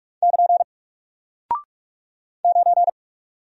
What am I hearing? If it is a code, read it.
Morse code, heard as a keyed tone, about 48 wpm, 700 Hz. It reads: <KN> 9
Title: Single Character (Letter:Number:Punctuation:Prosign) Alt 45wpm